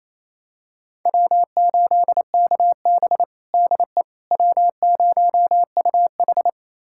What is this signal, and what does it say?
Morse code, heard as a keyed tone, about 27 wpm, 700 Hz. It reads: W8KB DE W0UH
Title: USA Callsign DE Callsign - 4 Characters - 28wpm